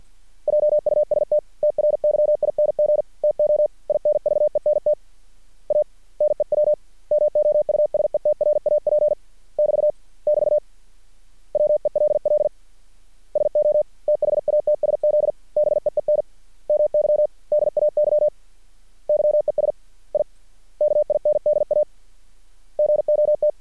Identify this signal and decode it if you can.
Morse code, heard as a keyed tone, about 50 wpm, 600 Hz. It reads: JUST TRYING TO INVENT A NEW MOUSETRAP <BT> <BT> WELL SO THATSZ BEEN MY DAY YES I KINDA GOT
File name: archived / CRSnet